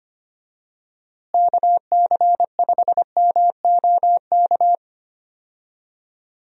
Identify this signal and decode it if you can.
Morse code, heard as a keyed tone, about 25 wpm, 700 Hz. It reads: KC5MOK
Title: USA Call Signs - 6 Characters - Repeated 3x - 25wpm